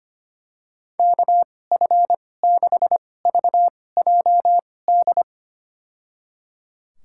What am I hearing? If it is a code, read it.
Morse code, heard as a keyed tone, about 25 wpm, spaced as for 18 wpm, 700 Hz. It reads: KF6VJD
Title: USA Call Signs - 6 Characters - 2x Character Spacing - 25wpm